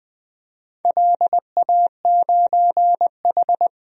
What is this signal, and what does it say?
Morse code, heard as a keyed tone, about 20 wpm, 700 Hz. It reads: LA9H
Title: DXCC-100 Call Signs - 4 Characters - 20wpm